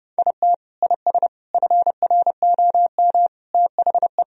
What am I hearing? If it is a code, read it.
Morse code, heard as a keyed tone, about 30 wpm, 700 Hz. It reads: IT IS FROM THE
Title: Sentences from Top 1000 Words - Review Part 1 - 30wpm